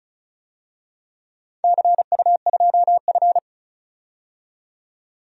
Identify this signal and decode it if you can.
Morse code, heard as a keyed tone, about 36 wpm, 700 Hz. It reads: CU2F
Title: DXCC-100 Call Signs - 4 Characters - 35wpm